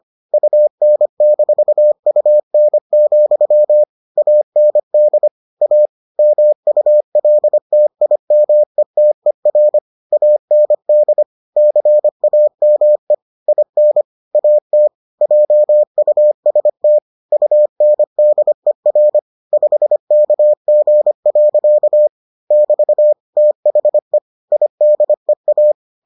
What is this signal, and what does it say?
Morse code, heard as a keyed tone, about 25 wpm, 600 Hz. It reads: UN-UN, AND A MULTIMETER AND CAME IN AT JUST UNDER 5KG. <BT> THE IDEA